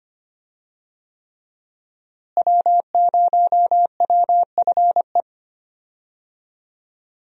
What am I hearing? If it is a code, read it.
Morse code, heard as a keyed tone, about 25 wpm, 700 Hz. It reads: W0WFE
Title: USA Call Signs - 5 Characters - Repeated 3x - 25wpm